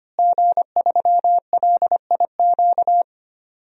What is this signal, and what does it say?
Morse code, heard as a keyed tone, about 25 wpm, 700 Hz. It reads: G3LIQ
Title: DXCC-100 Call Signs - 5 Characters - Repeated 3x - 25wpm